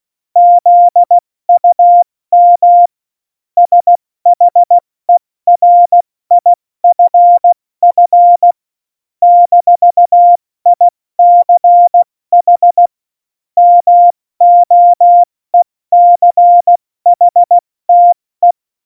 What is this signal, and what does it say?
Morse code, heard as a keyed tone, about 16 wpm, spaced as for 14 wpm, 700 Hz. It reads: ZUM SHERIFF - ICH MOECHTE
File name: Morsekurs / Witze / Witz 05. September 17, Tempo 70